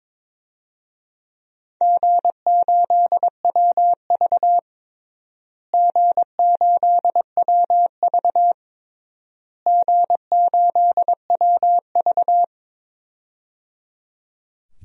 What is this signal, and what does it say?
Morse code, heard as a keyed tone, about 22 wpm, 700 Hz. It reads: G8WV G8WV G8WV
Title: DXCC-100 Call Signs - 4 Characters - Repeated 3x - 22wpm